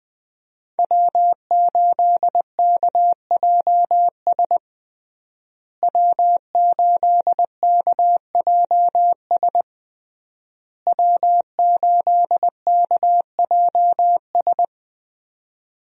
Morse code, heard as a keyed tone, about 21 wpm, 700 Hz. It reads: W8KJS W8KJS W8KJS
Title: USA Call Signs - 5 Characters - Repeated 3x - 20wpm